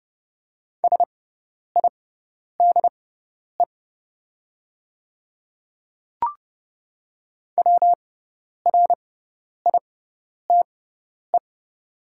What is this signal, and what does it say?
Morse code, heard as a keyed tone, about 30 wpm, spaced as for 10 wpm, 700 Hz. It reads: SIDE WRITE
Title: 6x Character Spacing - Top 200 Words - 30wpm